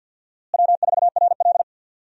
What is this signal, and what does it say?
Morse code, heard as a keyed tone, about 51 wpm, 700 Hz. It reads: W4RL
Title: USA Call Signs - 4 Characters - Repeated 2x - 50wpm